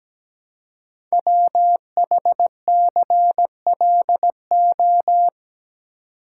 Morse code, heard as a keyed tone, about 17 wpm, 700 Hz. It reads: WHCLO
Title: Code Groups - 5 Letter:Numbers - 17wpm